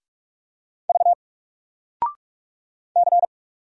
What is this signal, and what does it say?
Morse code, heard as a keyed tone, about 44 wpm, 700 Hz. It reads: V C
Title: Single Character (Letter:Number:Punctuation:Prosign) 45wpm